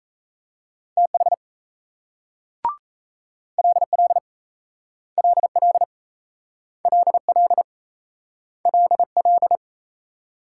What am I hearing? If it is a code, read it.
Morse code, heard as a keyed tone, about 38 wpm, 700 Hz. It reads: TH LL LL LL LL